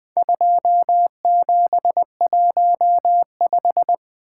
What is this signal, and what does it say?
Morse code, heard as a keyed tone, about 21 wpm, 700 Hz. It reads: 2715
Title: Four Letter-Number Combinations - 20wpm